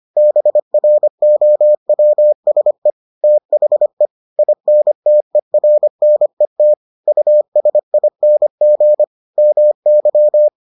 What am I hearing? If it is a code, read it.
Morse code, heard as a keyed tone, about 25 wpm, 600 Hz. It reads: BROWSE THE INTERNET USING MY